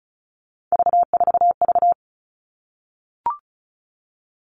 Morse code, heard as a keyed tone, about 34 wpm, 700 Hz. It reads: V4V
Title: All Cut Numbers - 3-Digit 35wpm